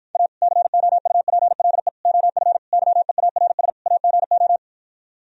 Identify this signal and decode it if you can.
Morse code, heard as a keyed tone, about 52 wpm, 700 Hz. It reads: A COUPLE OF YEARS AGO